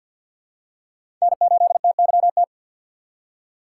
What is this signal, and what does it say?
Morse code, heard as a keyed tone, about 53 wpm, 700 Hz. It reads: N8TYT
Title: USA Call Signs - 5 Characters - 3x Character Spacing - 50wpm